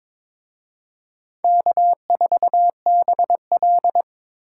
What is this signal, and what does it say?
Morse code, heard as a keyed tone, about 26 wpm, 700 Hz. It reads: K4BL
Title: Speed Racing - USA 1x2 Call Signs 22wpm